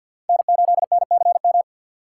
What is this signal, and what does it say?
Morse code, heard as a keyed tone, about 51 wpm, 690 Hz. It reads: N8NCM
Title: USA Call Signs - 5 Characters - 3x Character Spacing - 50wpm